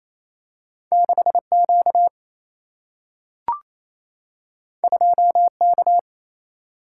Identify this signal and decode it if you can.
Morse code, heard as a keyed tone, about 28 wpm, 700 Hz. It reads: 6Q 2K
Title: Single-Letter Single-Number 28wpm